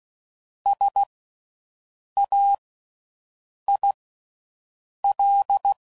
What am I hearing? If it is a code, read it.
Morse code, heard as a keyed tone, about 16 wpm, spaced as for 6 wpm, 800 Hz. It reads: SAIL
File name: Slow With Faster Ings